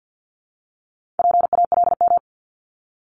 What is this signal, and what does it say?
Morse code, heard as a keyed tone, about 50 wpm, 700 Hz. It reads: PALM
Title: Multi-letter Phonemes Repeated 3x 50wpm